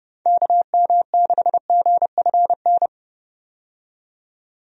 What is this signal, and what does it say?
Morse code, heard as a keyed tone, about 30 wpm, 700 Hz. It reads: KM6GFN
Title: USA Call Signs - 6 Characters - 30wpm